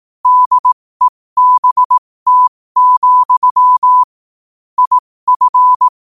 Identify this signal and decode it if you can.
Morse code, heard as a keyed tone, about 18 wpm, spaced as for 15 wpm, 1000 Hz. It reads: DEBT, IF
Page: New quotes every day in morse code at 15 Words per minute.